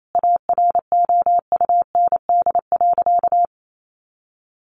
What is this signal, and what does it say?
Morse code, heard as a keyed tone, about 28 wpm, 700 Hz. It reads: AROUND.
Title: Sentences from Top 600 Words 28wpm